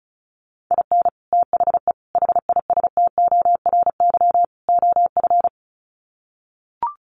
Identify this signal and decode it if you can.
Morse code, heard as a keyed tone, about 34 wpm, 700 Hz. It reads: IN THE HISTORY OF
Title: Sets of 4 Words 35wpm